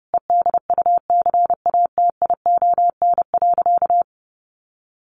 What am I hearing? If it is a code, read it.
Morse code, heard as a keyed tone, about 30 wpm, 700 Hz. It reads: EDUCATION.
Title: Sentences from Top 1500 Words 30wpm